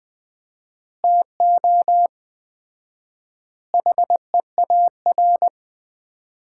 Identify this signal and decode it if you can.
Morse code, heard as a keyed tone, about 20 wpm, 700 Hz. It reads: TO HEAR
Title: Common QSO Phrases - 4x Word-Spacing - 20wpm